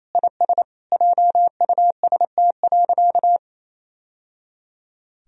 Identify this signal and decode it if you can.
Morse code, heard as a keyed tone, about 28 wpm, 700 Hz. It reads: IS JUST.
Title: Sentences from Top 1000 Words - Review Part 1 - 28wpm